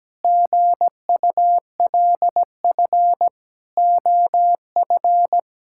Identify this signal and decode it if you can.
Morse code, heard as a keyed tone, about 17 wpm, 700 Hz. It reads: GULF OF
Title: Pangram Challenge v2 - 17wpm